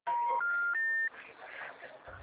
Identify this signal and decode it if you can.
Special information tone, with noise.